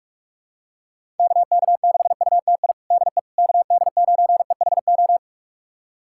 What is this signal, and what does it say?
Morse code, heard as a keyed tone, about 46 wpm, 700 Hz. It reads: KK6UTI DE KD9EHO
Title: USA Callsign DE Callsign 45wpm